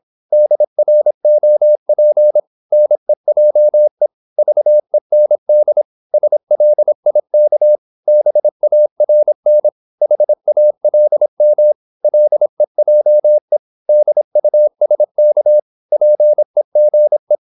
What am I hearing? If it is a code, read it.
Morse code, heard as a keyed tone, about 26 wpm, 600 Hz. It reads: DROP NEJE VEND SLIK BARN HALM LEJE DUSK PEGE
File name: Korte ord DK 26wpm